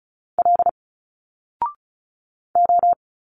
Morse code, heard as a keyed tone, about 36 wpm, 700 Hz. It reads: L O
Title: Single Letters 35wpm